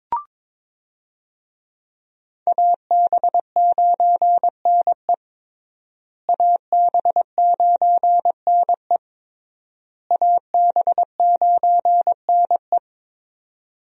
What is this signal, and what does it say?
Morse code, heard as a keyed tone, about 22 wpm, 700 Hz. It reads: AB9NE AB9NE AB9NE
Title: USA Call Signs - 5 Characters - Repeated 3x - 22wpm